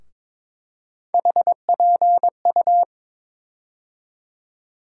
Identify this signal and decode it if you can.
Morse code, heard as a keyed tone, about 22 wpm, 700 Hz. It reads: HPU
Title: USA Call Sign Postfixes - 3 Letters 22wpm